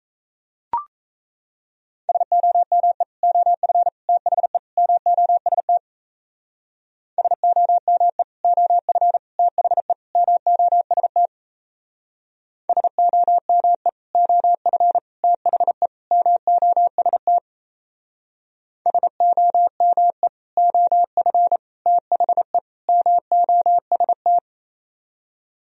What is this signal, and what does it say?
Morse code, heard as a keyed tone, about 37 wpm, 700 Hz. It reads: SOME OF THE MOST SOME OF THE MOST SOME OF THE MOST SOME OF THE MOST